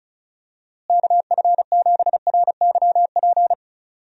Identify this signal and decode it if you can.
Morse code, heard as a keyed tone, about 36 wpm, 700 Hz. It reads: KF7RYP
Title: USA Call Signs - 6 Characters - Repeated 3x - 35wpm